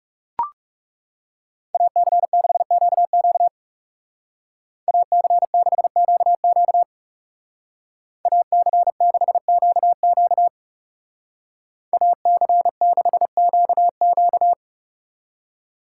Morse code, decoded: AC6QQ AC6QQ AC6QQ AC6QQ

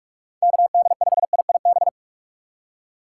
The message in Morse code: KD5IIB

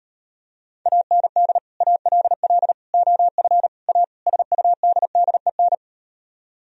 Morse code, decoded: AND ALL OF A SUDDEN

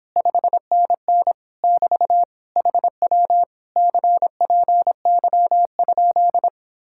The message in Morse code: 5NN <BT> HW CPY?

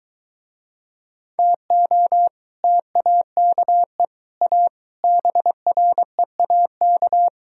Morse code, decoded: TO TAKE A BREAK

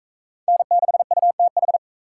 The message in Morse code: N6UTH

